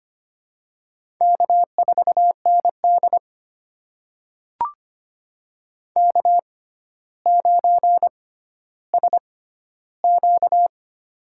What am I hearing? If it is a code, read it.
Morse code, heard as a keyed tone, about 25 wpm, spaced as for 13 wpm, 700 Hz. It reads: K4ND K9SQ
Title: USA Call Signs - 4 Characters - 6x Character Spacing - 25wpm